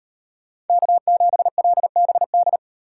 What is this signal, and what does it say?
Morse code, heard as a keyed tone, about 38 wpm, 700 Hz. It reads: K7LBD